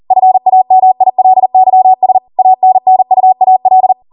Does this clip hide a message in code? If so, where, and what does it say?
Yes: Morse code from 0.1 to 3.9 s, heard as a keyed tone, about 40 wpm, 750 Hz. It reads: FAMILYS ANNUAL